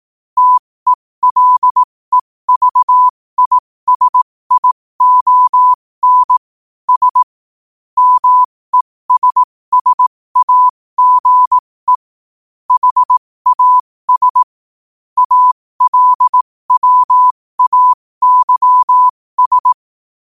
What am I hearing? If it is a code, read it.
Morse code, heard as a keyed tone, about 18 wpm, spaced as for 15 wpm, 1000 Hz. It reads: TELEVISION S MESSAGE HAS ALWAYS
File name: qotdfiles / 15wpm